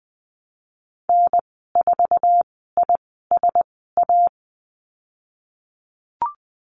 Morse code, heard as a keyed tone, about 20 wpm, 700 Hz. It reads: N 4 I S A
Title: 048 Lesson 9g - Number 4 - Call Signs 20wpm